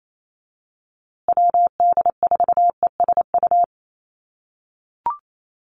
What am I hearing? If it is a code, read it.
Morse code, heard as a keyed tone, about 28 wpm, 690 Hz. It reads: WD4ESU
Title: USA Call Signs Repeated 3x 28wpm